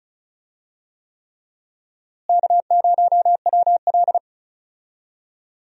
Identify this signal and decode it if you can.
Morse code, heard as a keyed tone, about 36 wpm, 700 Hz. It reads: K0WL
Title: USA Call Signs - 4 Characters - Repeated 3x - 35wpm